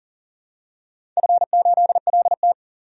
Morse code, heard as a keyed tone, about 40 wpm, 700 Hz. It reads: F8LT
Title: DXCC-100 Call Signs - 4 Characters - 40wpm